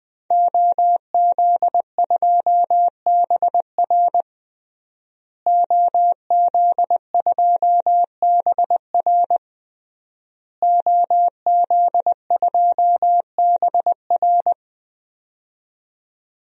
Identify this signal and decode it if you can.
Morse code, heard as a keyed tone, about 20 wpm, 700 Hz. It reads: OZ2BR OZ2BR OZ2BR
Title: DXCC-100 Call Signs - 5 Characters - Repeated 3x - 20wpm